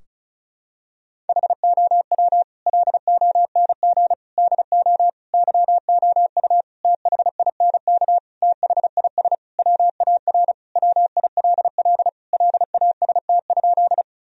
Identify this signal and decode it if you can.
Morse code, heard as a keyed tone, about 34 wpm, 700 Hz. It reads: HOW LONG DO YOU THINK THIS WAR WILL LAST?
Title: Sentences from Top 1000 Words - Review Part 2 - 35wpm